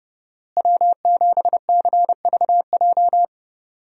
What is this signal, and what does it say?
Morse code, heard as a keyed tone, about 31 wpm, 700 Hz. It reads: W7CVJ
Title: USA Call Signs - 5 Characters - Repeated 2x - 30wpm